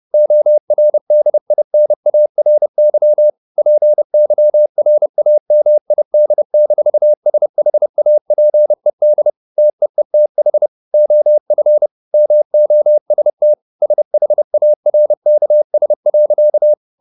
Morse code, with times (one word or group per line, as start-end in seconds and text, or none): 0.1-3.3 ORDINARY
3.6-9.3 PYRAMID-SHAPED
9.6-10.7 TEETH
10.9-11.9 OF
12.1-13.5 MOST
13.8-16.7 SHARKS.